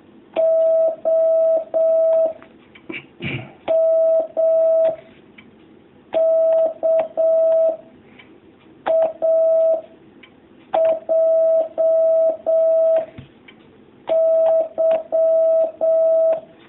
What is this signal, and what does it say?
Morse code, heard as a keyed tone, about 7 wpm, 650 Hz. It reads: O M K A J Y
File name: jh8cbh / 2019cw